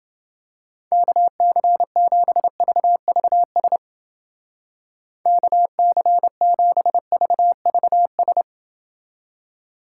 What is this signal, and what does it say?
Morse code, heard as a keyed tone, about 28 wpm, 700 Hz. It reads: KC7VVS KC7VVS